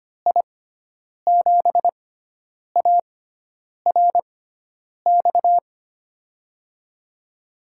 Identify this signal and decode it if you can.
Morse code, heard as a keyed tone, about 25 wpm, spaced as for 9 wpm, 700 Hz. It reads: I7ARX
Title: USA Call Signs - 6 Characters - 6x Character Spacing - 25wpm